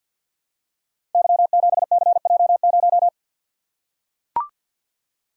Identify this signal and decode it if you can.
Morse code, heard as a keyed tone, about 45 wpm, 700 Hz. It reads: Y7CJ0
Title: Code Groups - 5 Letter:Numbers - 50wpm